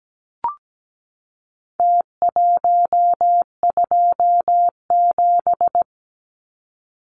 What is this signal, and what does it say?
Morse code, heard as a keyed tone, about 17 wpm, 700 Hz. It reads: T127
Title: Numbers - Contest Style - 4-Digit 17wpm